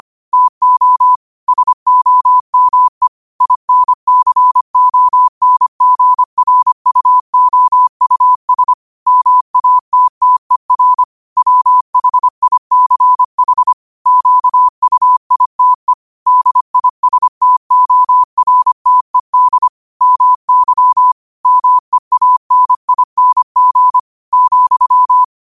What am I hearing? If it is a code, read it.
Morse code, heard as a keyed tone, about 25 wpm, 1000 Hz. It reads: TO SOME INCONGRUOUS MATTER WHICH QUITE DISTORTED MY MEANING,
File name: qotdfiles / 25wpm